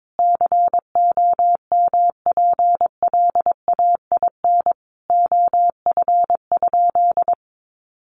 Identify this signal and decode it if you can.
Morse code, heard as a keyed tone, about 22 wpm, 700 Hz. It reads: COMPLAIN OF?
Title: Sentences from Top 1300 Words 22wpm